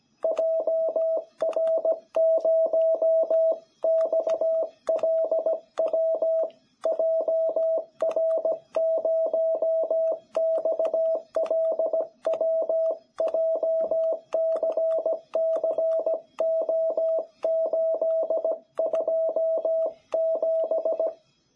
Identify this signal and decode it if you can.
Morse code, heard as a keyed tone, about 17 wpm, 660 Hz. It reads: JR0XLW JR0XLW JCCO827
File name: jh8cbh / 2021cw